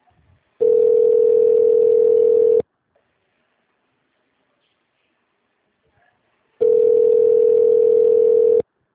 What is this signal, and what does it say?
Ringback tone (North America), 440 + 480 Hz.